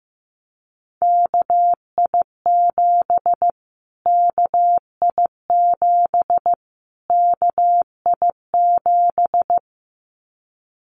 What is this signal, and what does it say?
Morse code, heard as a keyed tone, about 15 wpm, 700 Hz. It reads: KI7 KI7 KI7
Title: Callsign Prefixes for Top 25 DXCC - Repeated 3x - 15wpm